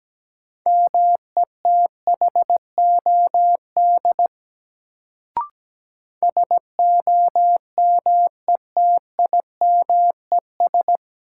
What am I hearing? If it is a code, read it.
Morse code, heard as a keyed tone, about 17 wpm, 700 Hz. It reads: METHOD SOMETIMES
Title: Sentences from Top 600 Words - New Word Drill - 17wpm